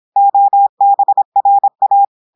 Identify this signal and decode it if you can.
Morse code, heard as a keyed tone, about 26 wpm, 800 Hz. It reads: OBRA
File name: audios / portugais / portugais texte en clair 1 vitesse 26 mots minutes